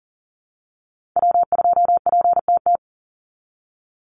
Morse code, heard as a keyed tone, about 40 wpm, 700 Hz. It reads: W2PTT
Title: Call Signs - 40wpm